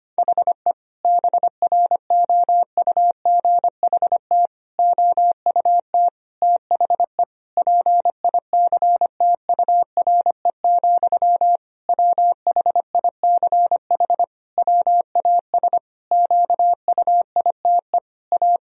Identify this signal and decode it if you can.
Morse code, heard as a keyed tone, about 25 wpm, 700 Hz. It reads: HE BROUGHT OUT THE PICTURE, WHICH WAS QUITE A
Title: Sentences from Top 1000 Words - Review Part 3 - 25wpm